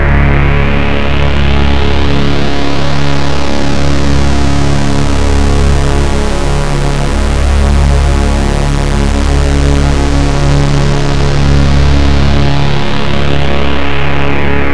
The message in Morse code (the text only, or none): none